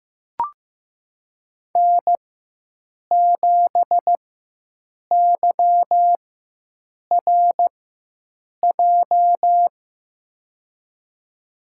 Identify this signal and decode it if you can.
Morse code, heard as a keyed tone, about 15 wpm, spaced as for 7 wpm, 700 Hz. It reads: N7YRJ
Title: USA Call Signs - 5 Characters - 4x Character Spacing - 15wpm